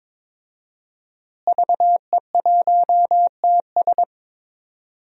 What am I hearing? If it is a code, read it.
Morse code, heard as a keyed tone, about 22 wpm, 700 Hz. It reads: VE1TS
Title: DXCC-100 Call Signs - 5 Characters - Repeated 3x - 22wpm